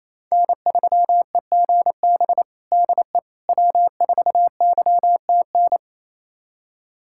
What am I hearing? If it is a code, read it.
Morse code, heard as a keyed tone, about 28 wpm, 700 Hz. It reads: N3EGB DE W4YTN
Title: USA Callsign DE Callsign - 5 Characters - 28wpm